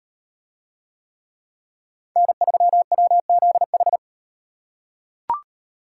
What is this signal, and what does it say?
Morse code, heard as a keyed tone, about 38 wpm, 700 Hz. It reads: N3WZH